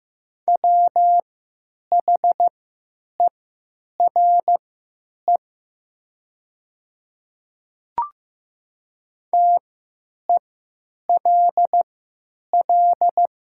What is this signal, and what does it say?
Morse code, heard as a keyed tone, about 15 wpm, spaced as for 9 wpm, 700 Hz. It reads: WHERE TELL
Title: 3x Character Spacing - Top 200 Words - 15wpm